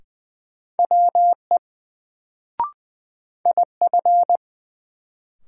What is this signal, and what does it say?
Morse code, heard as a keyed tone, about 20 wpm, 700 Hz. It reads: WE IF
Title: Sets of 2 Words - New Word Drill - 20wpm